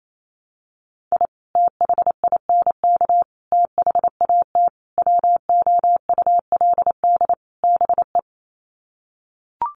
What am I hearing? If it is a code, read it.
Morse code, heard as a keyed tone, about 28 wpm, 700 Hz. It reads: I THINK THAT WOULD BE
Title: Sets of 5 Words Encore2 28wpm